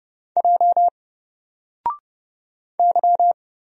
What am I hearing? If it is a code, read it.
Morse code, heard as a keyed tone, about 31 wpm, 700 Hz. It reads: J Y
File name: Single-Letter-Number-30wpm